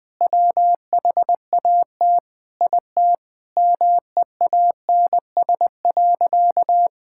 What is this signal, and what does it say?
Morse code, heard as a keyed tone, about 20 wpm, 700 Hz. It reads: WHAT IT MEANS.
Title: Sentences with 5-Word Phrases - Spoken Beginning - 20wpm